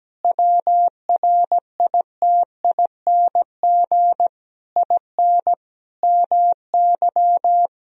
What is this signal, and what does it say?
Morse code, heard as a keyed tone, about 17 wpm, 700 Hz. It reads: WRITING IN MY
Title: Sentences from Top 1000 Words - Review Part 1 - 17wpm